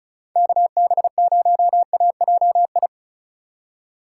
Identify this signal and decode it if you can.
Morse code, heard as a keyed tone, about 36 wpm, 700 Hz. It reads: KB0AJI
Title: USA Call Signs - 6 Characters - 4x Character Spacing - 35wpm